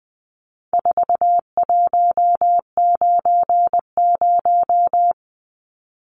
Morse code, heard as a keyed tone, about 20 wpm, 700 Hz. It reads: 4190
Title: Numbers - 4-Digit 20wpm